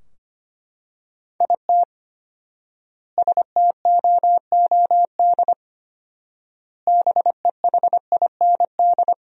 Morse code, heard as a keyed tone, about 25 wpm, 700 Hz. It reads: IT STOOD BEHIND
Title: 4x Word-Spacing - Sentences from Top 500 Words - 25wpm